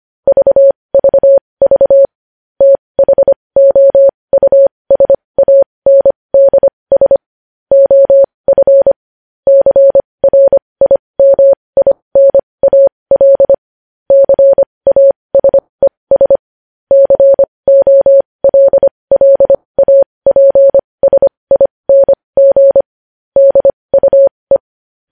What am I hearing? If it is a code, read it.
Morse code, heard as a keyed tone, about 25 wpm, spaced as for 20 wpm, 550 Hz. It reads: VVV THOUSANDS OF CRIMINAL CASES COLLAPSING DUE
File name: podcast / News headlines-1x-25x20